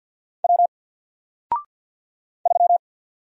Morse code, heard as a keyed tone, about 50 wpm, 700 Hz. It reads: W 3
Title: Single Letter-Number 50wpm